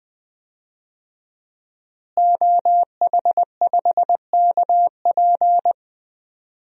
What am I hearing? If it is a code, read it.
Morse code, heard as a keyed tone, about 20 wpm, 700 Hz. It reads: OH5KP
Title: DXCC-100 Call Signs - 5 Characters - Repeated 2x - 20wpm